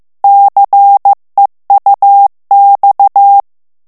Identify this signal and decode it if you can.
Morse code, heard as a keyed tone, about 15 wpm, 800 Hz. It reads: CEUX
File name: morse mp3 / FRANCAIS TEXTE CLAIR 2 VITESSE 14 MOTS MINUTES